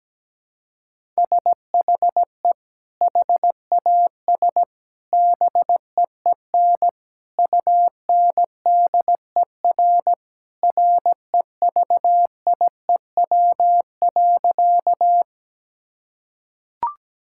Morse code, heard as a keyed tone, about 17 wpm, 700 Hz. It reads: SHE HAS BEEN UNDER REVIEW.
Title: Sentences from Top 1000 Words - Review Part 6 - 17wpm